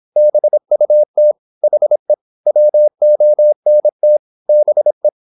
Morse code, heard as a keyed tone, about 26 wpm, 600 Hz. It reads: BUT HE WONT BE
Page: Never 26wpm | CW med Gnister